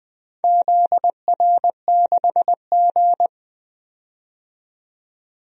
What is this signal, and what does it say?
Morse code, heard as a keyed tone, about 20 wpm, 700 Hz. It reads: ZR6G
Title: DXCC-100 Call Signs - 4 Characters - 20wpm